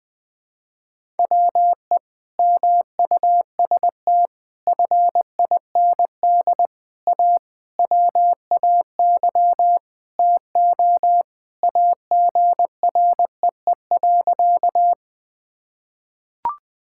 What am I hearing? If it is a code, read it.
Morse code, heard as a keyed tone, about 21 wpm, 700 Hz. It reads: WE MUST FIND A WAY TO AGREE.
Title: Sentences with 5-Word Phrases - 20wpm